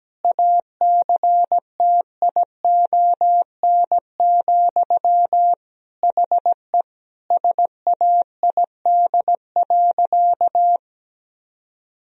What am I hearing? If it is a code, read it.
Morse code, heard as a keyed tone, about 17 wpm, 700 Hz. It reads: ACTION, HE SAID.
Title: Sentences from Top 1000 Words - Review Part 4 - 17wpm